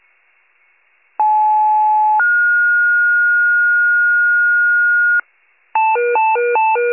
Two-tone page, heard as a one-second tone, then a three-second tone, about 860 then 1430 Hz, with noise. It is heard from 1.2 to 5.2 s.